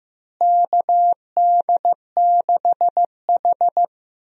Morse code, heard as a keyed tone, about 15 wpm, 700 Hz. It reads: KD6H
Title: USA Call Signs - 4 Characters - Repeated 3x - 15wpm